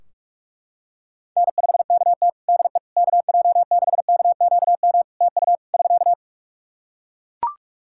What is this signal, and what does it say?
Morse code, heard as a keyed tone, about 42 wpm, 700 Hz. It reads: N5KT DE KJ6KQM TU <SK>
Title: Practice QSO - Exchange 4 - Ending - 45wpm